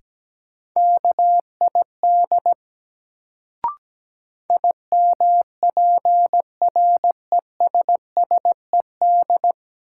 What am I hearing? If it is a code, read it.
Morse code, heard as a keyed tone, about 17 wpm, 700 Hz. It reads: KID IMPRESSED
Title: Sentences from Top 1600 Words - New Word Drill - 17wpm